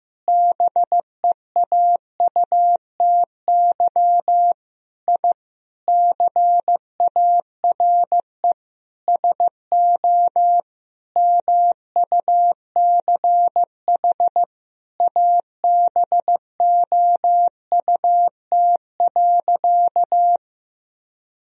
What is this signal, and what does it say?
Morse code, heard as a keyed tone, about 15 wpm, 700 Hz. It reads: BEAUTY I CARE SO MUCH ABOUT.
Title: Sentences from Top 1000 Words - Review Part 2 - 15wpm